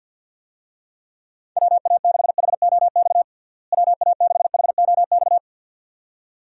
Morse code, heard as a keyed tone, about 50 wpm, 700 Hz. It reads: WA6HOX WA6HOX
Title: USA Call Signs - 6 Characters - Repeated 2x - 50wpm